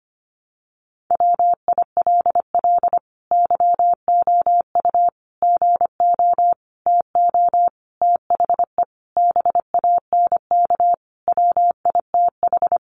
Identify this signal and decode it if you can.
Morse code, heard as a keyed tone, about 25 wpm, 700 Hz. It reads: WILL YOU GO TO THE BANK WITH
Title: Sentences from Top 900 Words 25wpm